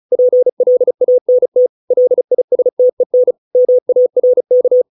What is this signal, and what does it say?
Morse code, heard as a keyed tone, about 34 wpm, 500 Hz. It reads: PLANT LISTEN MARK